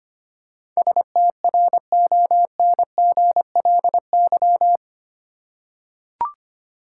Morse code, heard as a keyed tone, about 25 wpm, 700 Hz. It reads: STRONGLY
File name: Eight Letter Words - 25wpm